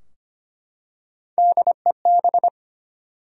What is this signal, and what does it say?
Morse code, heard as a keyed tone, about 25 wpm, 700 Hz. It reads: DEB
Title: USA Common Names - Max 3 Letters - 25wpm